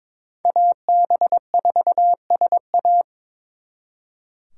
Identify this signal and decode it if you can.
Morse code, heard as a keyed tone, about 26 wpm, 700 Hz. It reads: AB4SA
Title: Speed Racing - USA Call Signs 22wpm